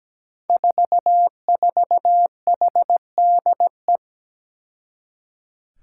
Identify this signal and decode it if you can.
Morse code, heard as a keyed tone, about 17 wpm, 700 Hz. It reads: 44HDE
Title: Five Challenging Letter-Number Combinations 17wpm